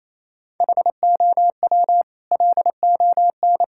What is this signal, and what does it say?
Morse code, heard as a keyed tone, about 28 wpm, 700 Hz. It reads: HOW LON
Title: Sentences from Top 1600 Words 28wpm